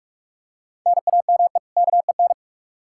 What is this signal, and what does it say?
Morse code, heard as a keyed tone, about 45 wpm, 700 Hz. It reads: NAME KEN
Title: Practice QSO - Exchange 1 - Beginning - 45wpm